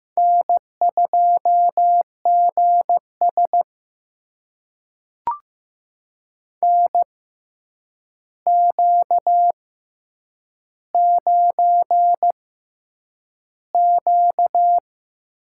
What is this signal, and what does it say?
Morse code, heard as a keyed tone, about 15 wpm, spaced as for 8 wpm, 700 Hz. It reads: N2GS NQ9Q
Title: USA Call Signs - 4 Characters - 6x Character Spacing - 15wpm